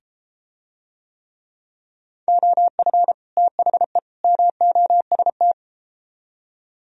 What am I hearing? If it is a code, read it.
Morse code, heard as a keyed tone, about 31 wpm, 700 Hz. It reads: OF THE MOST